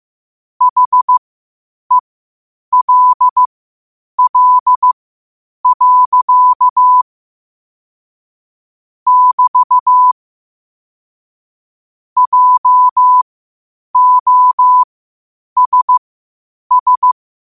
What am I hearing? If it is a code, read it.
Morse code, heard as a keyed tone, about 15 wpm, spaced as for 8 wpm, 1000 Hz. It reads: HELL. <BT> JOSS
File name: qotdfiles / 8wpm